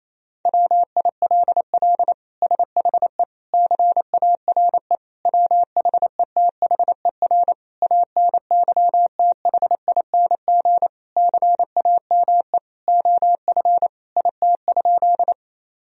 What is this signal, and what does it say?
Morse code, heard as a keyed tone, about 28 wpm, 700 Hz. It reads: WILL SHE CARE WHETHER ANYTHING CAME OF IT?
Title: Sentences from Top 1000 Words - Review Part 3 - 28wpm